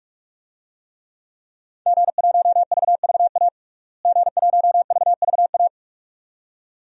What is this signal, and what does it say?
Morse code, heard as a keyed tone, about 46 wpm, 700 Hz. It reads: G1VVA G1VVA
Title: DXCC-100 Call Signs - 5 Characters - Repeated 2x - 45wpm